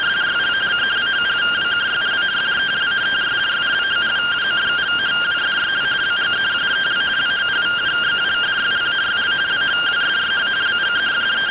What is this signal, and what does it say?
Two-tone teleprinter signal, about 170 Hz shift.